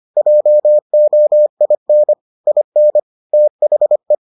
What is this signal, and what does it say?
Morse code, heard as a keyed tone, about 25 wpm, 600 Hz. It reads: JOIN IN THE